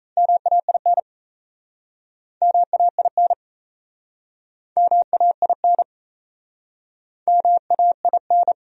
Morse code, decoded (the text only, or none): MAIN MAIN MAIN MAIN